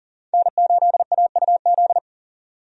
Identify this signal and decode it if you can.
Morse code, heard as a keyed tone, about 40 wpm, 700 Hz. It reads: N8AUZ
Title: USA Call Signs - 5 Characters - 40wpm